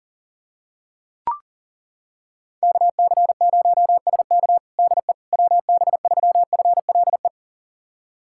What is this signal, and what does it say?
Morse code, heard as a keyed tone, about 40 wpm, 700 Hz. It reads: KC0SK DE WB3FLE
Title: USA Callsign DE Callsign 40wpm